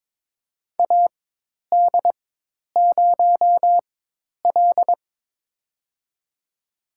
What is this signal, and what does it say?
Morse code, heard as a keyed tone, about 22 wpm, spaced as for 10 wpm, 700 Hz. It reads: AD0L
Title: USA Call Signs - 4 Characters - 4x Character Spacing - 22wpm